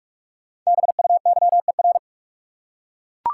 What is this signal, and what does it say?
Morse code, heard as a keyed tone, about 42 wpm, 700 Hz. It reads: BUYER
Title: Five Letter Words - 45wpm